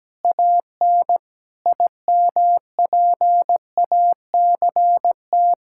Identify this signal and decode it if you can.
Morse code, heard as a keyed tone, about 17 wpm, 700 Hz. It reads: AN IMPACT
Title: Sentences from Top 1000 Words - Review Part 6 - 17wpm